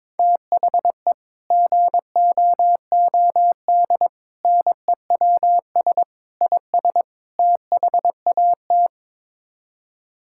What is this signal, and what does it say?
Morse code, heard as a keyed tone, about 22 wpm, 700 Hz. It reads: THE GOOD NEWS IS THAT
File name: practice / 22wpm / Build A Phrase - 5-Word Phrases - 22wpm